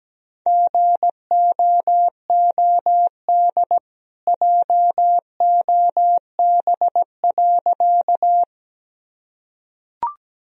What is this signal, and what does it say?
Morse code, heard as a keyed tone, about 17 wpm, 700 Hz. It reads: GOOD JOB.
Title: Sentences from Top 1000 Words - Review Part 3 - 17wpm